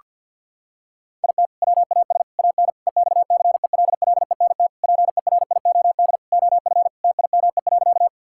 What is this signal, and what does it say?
Morse code, heard as a keyed tone, about 52 wpm, 700 Hz. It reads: IT WAS AN EXCELLENT PERIOD OF TIME.
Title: Sentences from Top 1000 Words - Review Part 5 - 50wpm